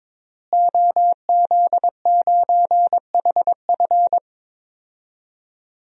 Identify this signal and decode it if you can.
Morse code, heard as a keyed tone, about 22 wpm, 700 Hz. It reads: OZ9HF
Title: DXCC-100 Call Signs - 5 Characters - Repeated 3x - 22wpm